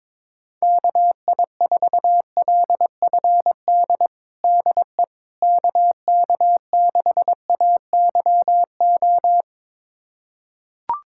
Morse code, heard as a keyed tone, about 22 wpm, 700 Hz. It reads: KI4LFD DE KK6AYO
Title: USA Callsign DE Callsign 22wpm